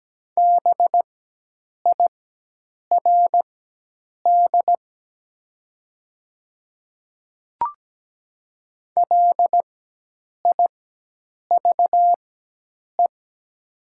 Morse code, heard as a keyed tone, about 17 wpm, spaced as for 8 wpm, 700 Hz. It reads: BIRD LIVE
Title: 4x Character Spacing - Top 500 Words - 17wpm